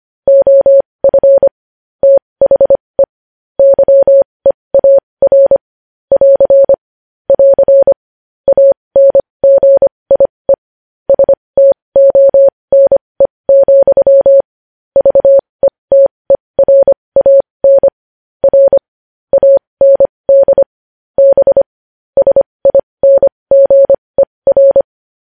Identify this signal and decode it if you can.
Morse code, heard as a keyed tone, about 25 wpm, spaced as for 20 wpm, 550 Hz. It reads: OF THE YEAR <AR> <AR> ANGIE STONE, VETERAN R AND B SINGER
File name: podcast / News headlines-1x-25x20